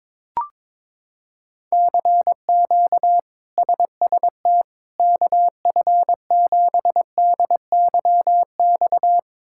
Morse code, heard as a keyed tone, about 21 wpm, 700 Hz. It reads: CQ SST KF7DYX
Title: K1USN SST QSO Practice - 22wpm